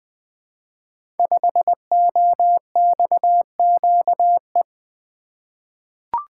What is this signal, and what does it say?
Morse code, heard as a keyed tone, about 20 wpm, 700 Hz. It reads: 5OXQE
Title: Code Groups - 5 Letter:Numbers - 20wpm